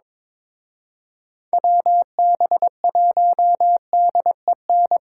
Morse code, heard as a keyed tone, about 24 wpm, 700 Hz. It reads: WB1DEN
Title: Speed Racing - USA Call Signs 22wpm